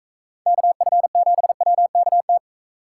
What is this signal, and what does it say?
Morse code, heard as a keyed tone, about 42 wpm, 700 Hz. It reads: KF7WKT